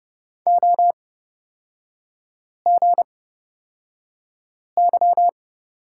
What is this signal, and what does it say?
Morse code, heard as a keyed tone, about 31 wpm, spaced as for 5 wpm, 700 Hz. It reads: OGY
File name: Lesson 36e - BT - Three Characters 30wpm